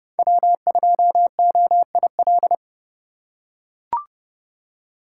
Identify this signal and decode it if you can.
Morse code, heard as a keyed tone, about 29 wpm, 700 Hz. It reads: W2OIL